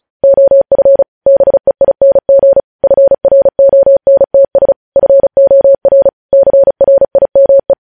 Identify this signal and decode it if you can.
Morse code, heard as a keyed tone, about 36 wpm, 550 Hz. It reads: OF BEING FRONTS FOR CRIME
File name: podcast / News headlines-1x-35x35